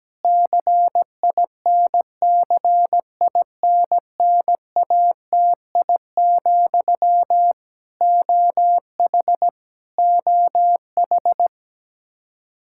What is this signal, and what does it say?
Morse code, heard as a keyed tone, about 17 wpm, 700 Hz. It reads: CINCINNATI, OH OH
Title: Practice QSO - Exchange 1 - Beginning - 17wpm